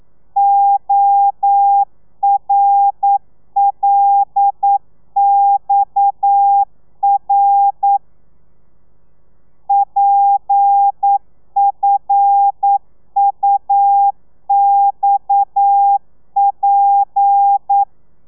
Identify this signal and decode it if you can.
Morse code, heard as a keyed tone, about 9 wpm, 800 Hz. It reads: ORLXR PFUXP
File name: telegrafia / i2tfj / Lez7 40